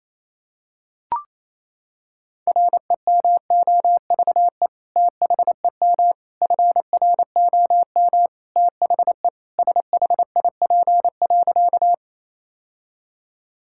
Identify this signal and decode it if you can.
Morse code, heard as a keyed tone, about 28 wpm, 700 Hz. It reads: REMOVE THEM FROM THE SHIP.
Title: Sentences from Top 1000 Words - Review Part 5 - 28wpm